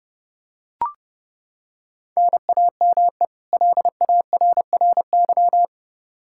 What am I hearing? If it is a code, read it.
Morse code, heard as a keyed tone, about 32 wpm, 700 Hz. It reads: NAME LARRY
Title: Practice QSO - Exchange 1 - Beginning - 30wpm